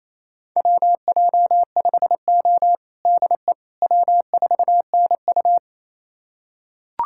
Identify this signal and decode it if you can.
Morse code, heard as a keyed tone, about 28 wpm, 700 Hz. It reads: WJ5O DE W4NU
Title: USA Callsign DE Callsign - 4 Characters - 28wpm